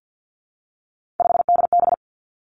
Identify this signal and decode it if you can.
Morse code, heard as a keyed tone, about 50 wpm, 700 Hz. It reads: 5DB
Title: Three Challenging Letter-Number Combinations 50wpm